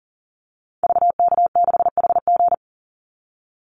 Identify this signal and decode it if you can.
Morse code, heard as a keyed tone, about 40 wpm, 700 Hz. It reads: VK6HG
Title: Call Signs Repeated 2x - 40wpm